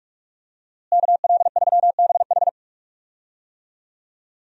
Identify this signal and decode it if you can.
Morse code, heard as a keyed tone, about 48 wpm, 700 Hz. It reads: KL3BH
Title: USA Call Signs - 5 Characters - Repeated 3x - 45wpm